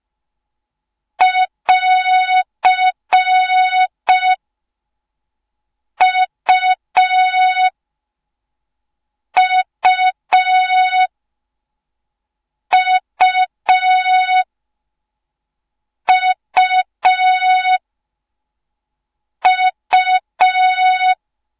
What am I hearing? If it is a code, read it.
Morse code, heard as a keyed tone, about 5 wpm, 3700 Hz. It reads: <AR> U U U U U